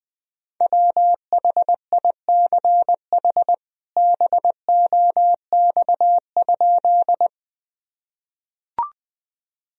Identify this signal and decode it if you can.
Morse code, heard as a keyed tone, about 20 wpm, 700 Hz. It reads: WHICH BOX?
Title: Sentences from Top 1000 Words - Review Part 2 - 20wpm